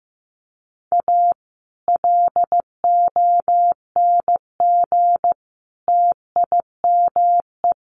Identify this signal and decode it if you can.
Morse code, heard as a keyed tone, about 15 wpm, 700 Hz. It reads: A LONG TIME
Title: Sets of 3 Words 15wpm